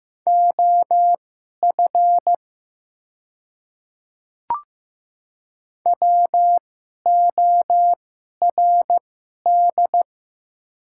Morse code, heard as a keyed tone, about 15 wpm, spaced as for 11 wpm, 700 Hz. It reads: OF WORD
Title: 2x Character Spacing - Top 100 Words - 15wpm